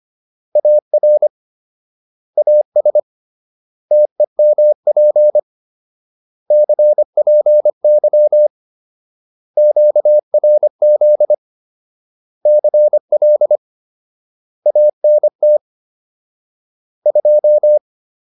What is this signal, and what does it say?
Morse code, heard as a keyed tone, about 25 wpm, 600 Hz. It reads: AR AS TEMP CPY QRZ CL ANT 2